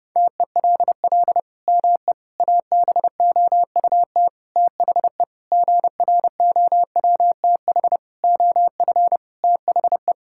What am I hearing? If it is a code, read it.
Morse code, heard as a keyed tone, about 30 wpm, 700 Hz. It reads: TELL ME ABOUT THE GROWTH OF THE
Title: Sentences from Top 1000 Words - Review Part 5 - 30wpm